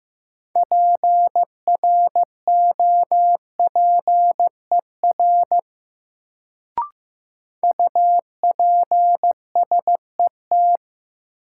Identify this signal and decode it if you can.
Morse code, heard as a keyed tone, about 15 wpm, 700 Hz. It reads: PROPER UPSET
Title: Sentences from Top 800 Words - New Word Drill - 15wpm